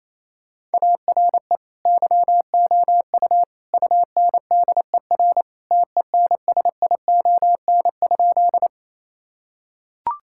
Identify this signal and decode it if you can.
Morse code, heard as a keyed tone, about 28 wpm, 700 Hz. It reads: ARE YOU UNDER TENSION?
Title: Sentences from Top 1600 Words 28wpm